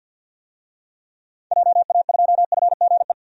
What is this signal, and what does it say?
Morse code, heard as a keyed tone, about 50 wpm, 700 Hz. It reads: JA2FGE
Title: DXCC-100 Call Signs - 6 Characters - Repeated 2x - 50wpm